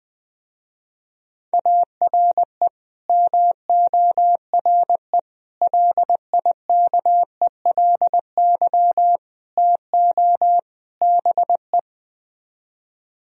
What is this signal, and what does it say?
Morse code, heard as a keyed tone, about 20 wpm, 700 Hz. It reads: ARE MORE LIKELY TO BE